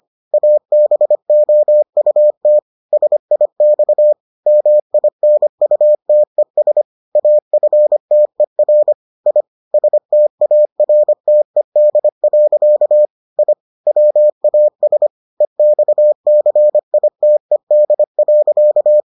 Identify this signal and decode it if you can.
Morse code, heard as a keyed tone, about 25 wpm, 600 Hz. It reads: ABOUT SIX MINUTES AFTER I STARTED. I WAS EXCITED.